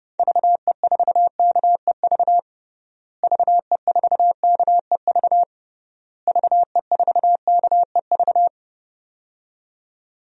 Morse code, heard as a keyed tone, about 32 wpm, 700 Hz. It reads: VE4KEV VE4KEV VE4KEV
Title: DXCC-100 Call Signs - 6 Characters - Repeated 3x - 30wpm